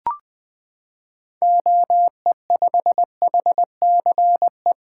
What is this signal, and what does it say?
Morse code, heard as a keyed tone, about 20 wpm, 700 Hz. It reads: OE5HCE
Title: 2023 SOTA Activator Call Signs - 20wpm